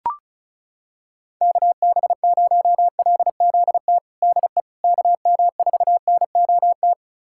Morse code, heard as a keyed tone, about 36 wpm, 700 Hz. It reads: KB0LZT DE KM4NOT
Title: USA Callsign DE Callsign - 6 Characters - 35wpm